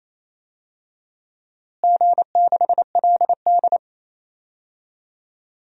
Morse code, heard as a keyed tone, about 28 wpm, 700 Hz. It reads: G6LD
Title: DXCC-100 Call Signs - 4 Characters - Repeated 2x - 28wpm